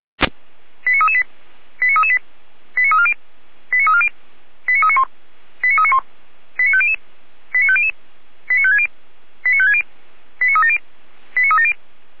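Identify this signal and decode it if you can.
Five-tone selective call (ZVEI), heard as quick runs of five tones, tones of about 80 ms. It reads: 89338 89338 89340 89340 89391 89391 89500 89500 89560 89560 89370 89370
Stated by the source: ZVEI five-tone call received on SDR, 455 channel.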